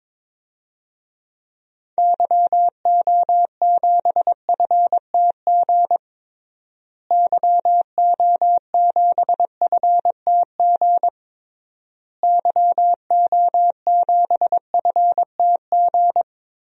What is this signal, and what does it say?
Morse code, heard as a keyed tone, about 22 wpm, 700 Hz. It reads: YO7FTG YO7FTG YO7FTG
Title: DXCC-100 Call Signs - 6 Characters - Repeated 3x - 22wpm